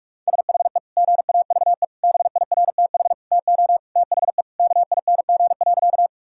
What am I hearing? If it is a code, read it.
Morse code, heard as a keyed tone, about 46 wpm, 700 Hz. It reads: SHE GAVE BIRTH TO THE KING.
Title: Sentences from Top 1000 Words - Review Part 5 - 45wpm